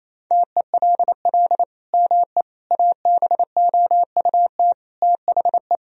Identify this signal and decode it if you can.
Morse code, heard as a keyed tone, about 28 wpm, 700 Hz. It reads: TELL ME ABOUT THE
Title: Sentences from Top 1000 Words - Review Part 2 - 28wpm